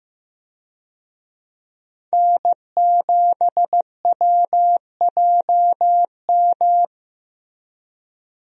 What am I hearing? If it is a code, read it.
Morse code, heard as a keyed tone, about 15 wpm, 700 Hz. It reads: N7WJM
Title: USA Call Signs - 5 Characters - Repeated 3x - 15wpm